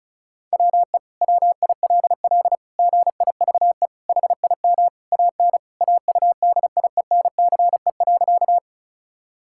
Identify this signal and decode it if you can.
Morse code, heard as a keyed tone, about 36 wpm, 700 Hz. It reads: WE WILL GIVE HIM AN AUDIENCE.
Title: Sentences from Top 1000 Words - Review Part 5 - 35wpm